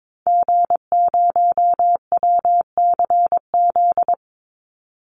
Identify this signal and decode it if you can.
Morse code, heard as a keyed tone, about 22 wpm, 700 Hz. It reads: G0WCZ
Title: Call Signs Repeated 3x - 22wpm